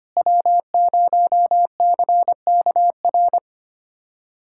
Morse code, heard as a keyed tone, about 25 wpm, 700 Hz. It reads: W0CKR
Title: USA Callsign DE Callsign - 5 Characters - 25wpm